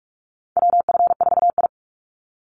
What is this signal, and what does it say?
Morse code, heard as a keyed tone, about 44 wpm, 700 Hz. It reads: WF4I
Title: NA SOTA Chaser Callsigns 45wpm